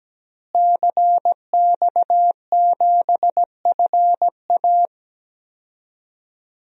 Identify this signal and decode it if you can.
Morse code, heard as a keyed tone, about 17 wpm, 700 Hz. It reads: CX7FA
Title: DXCC-100 Call Signs - 5 Characters - Repeated 3x - 17wpm